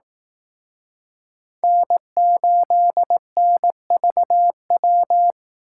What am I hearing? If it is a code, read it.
Morse code, heard as a keyed tone, about 18 wpm, 700 Hz. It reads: N8NVW
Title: Speed Racing - USA 1x3 Call Signs 15wpm